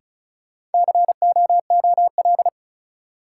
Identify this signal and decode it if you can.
Morse code, heard as a keyed tone, about 36 wpm, 700 Hz. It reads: COOL
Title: Sentences from Top 1200 Words - New Word Drill - 35wpm